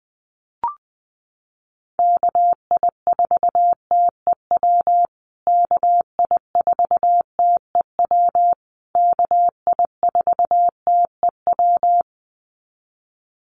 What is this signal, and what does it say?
Morse code, heard as a keyed tone, about 20 wpm, 700 Hz. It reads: KI4TEW KI4TEW KI4TEW
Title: USA Call Signs Repeated 3x 20wpm